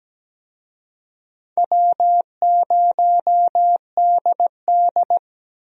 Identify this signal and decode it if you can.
Morse code, heard as a keyed tone, about 17 wpm, 700 Hz. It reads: W0DD
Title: USA Call Signs - 4 Characters - Repeated 2x - 17wpm